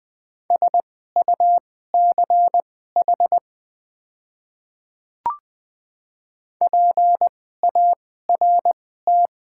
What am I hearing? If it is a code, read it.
Morse code, heard as a keyed tone, about 21 wpm, spaced as for 15 wpm, 700 Hz. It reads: SUCH PART
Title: 2x Character Spacing - Top 300 Words - 20wpm